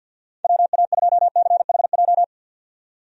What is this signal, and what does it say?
Morse code, heard as a keyed tone, about 46 wpm, 700 Hz. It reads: WA2CHJ
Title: USA Call Signs - 6 Characters - 50wpm